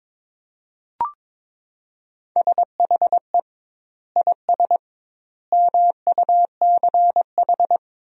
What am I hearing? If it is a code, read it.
Morse code, heard as a keyed tone, about 22 wpm, 700 Hz. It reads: SHE IS MUCH
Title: 2x Word-Spacing - Sentences from Top 400 Words - 22wpm